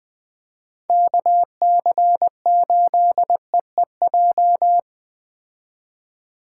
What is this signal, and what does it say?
Morse code, heard as a keyed tone, about 20 wpm, 700 Hz. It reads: KC8EEJ